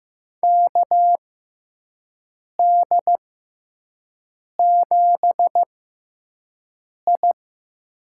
Morse code, heard as a keyed tone, about 15 wpm, spaced as for 5 wpm, 700 Hz. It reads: KD7I
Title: USA Call Signs - 6x Character Spacing - 15wpm